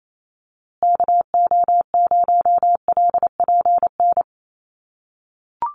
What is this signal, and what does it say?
Morse code, heard as a keyed tone, about 28 wpm, 690 Hz. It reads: KO0LPN
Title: USA Call Signs Repeated 3x 28wpm